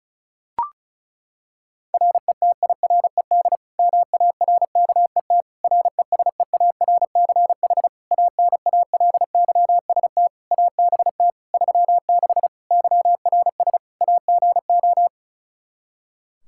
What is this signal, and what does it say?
Morse code, heard as a keyed tone, about 36 wpm, 700 Hz. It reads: RETIRED MARKET RESEARCH ANALYST ABT 36 YRS AGO
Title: Practice QSO - Exchange 3 - More Info - 35wpm